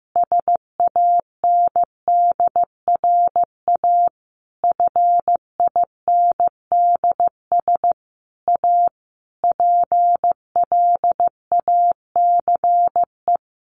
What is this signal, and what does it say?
Morse code, heard as a keyed tone, about 15 wpm, 700 Hz. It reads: SANDRA FINDS A PLACE
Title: Easy Phrases 15wpm